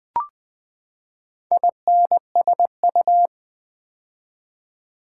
Morse code, heard as a keyed tone, about 20 wpm, 700 Hz. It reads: INSU
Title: Build A Word - 4 Letters At A Time - 12-Letter Words - 20wpm